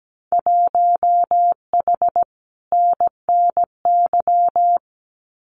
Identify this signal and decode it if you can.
Morse code, heard as a keyed tone, about 17 wpm, 700 Hz. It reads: 1H NNY
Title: Winter Field Day Exchanges 17wpm